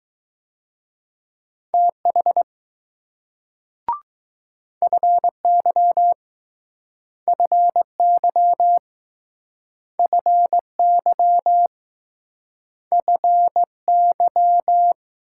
Morse code, heard as a keyed tone, about 20 wpm, 700 Hz. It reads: TH FY FY FY FY